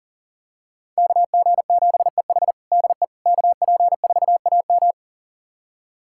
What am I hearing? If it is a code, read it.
Morse code, heard as a keyed tone, about 37 wpm, 700 Hz. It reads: KG7EH DE KP4AM
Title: USA Callsign DE Callsign - 5 Characters - 40wpm